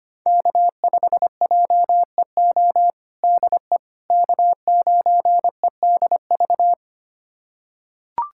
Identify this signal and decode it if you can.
Morse code, heard as a keyed tone, about 25 wpm, 700 Hz. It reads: K5JEO DE K9EDV
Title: USA Callsign DE Callsign - 5 Characters - 25wpm